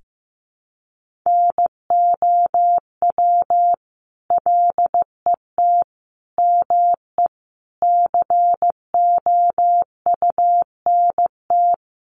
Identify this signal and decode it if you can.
Morse code, heard as a keyed tone, about 15 wpm, 700 Hz. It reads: NOW LET ME COUNT
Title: Sentences from Top 1500 Words 15wpm